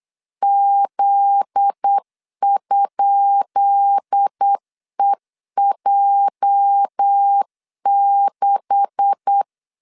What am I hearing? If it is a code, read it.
Morse code, heard as a keyed tone, about 8 wpm, 800 Hz. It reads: Z?EJ6
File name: LEZIONI / 10 SIGNS / 10B INT 8wpm 3534